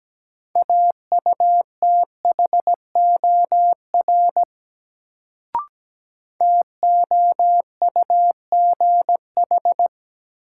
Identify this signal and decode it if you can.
Morse code, heard as a keyed tone, about 17 wpm, 700 Hz. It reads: AUTHOR TOUGH
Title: Sentences from Top 1200 Words - New Word Drill Alt - 17wpm